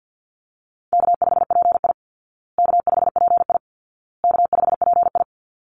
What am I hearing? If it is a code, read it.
Morse code, heard as a keyed tone, about 48 wpm, 700 Hz. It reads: K5PI K5PI K5PI
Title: Easy Call Signs Repeated 3x - 50wpm